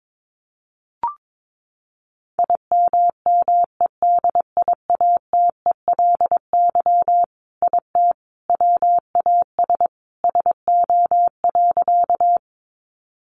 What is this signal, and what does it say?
Morse code, heard as a keyed tone, about 22 wpm, 700 Hz. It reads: IMMEDIATELY IT WAS SO.
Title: Sentences from Top 1100 Words 22wpm